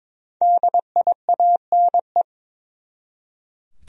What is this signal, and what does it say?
Morse code, heard as a keyed tone, about 22 wpm, 700 Hz. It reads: DIANE
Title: USA Common Names - Max 5 Letters - 22wpm